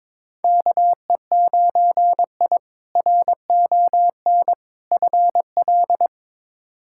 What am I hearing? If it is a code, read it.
Morse code, heard as a keyed tone, about 22 wpm, 700 Hz. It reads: KE9I RON FL
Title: K1USN SST QSO Practice - 22wpm